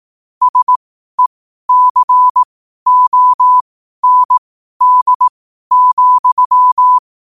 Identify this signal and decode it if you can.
Morse code, heard as a keyed tone, about 18 wpm, spaced as for 12 wpm, 1000 Hz. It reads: SECOND,
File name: qotdfiles / 12wpm